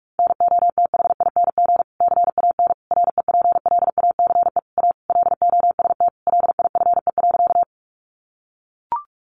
Morse code, heard as a keyed tone, about 44 wpm, 700 Hz. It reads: NOTHING CAN REPLACE A LOST LIFE.
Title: Sentences from Top 1400 Words 45wpm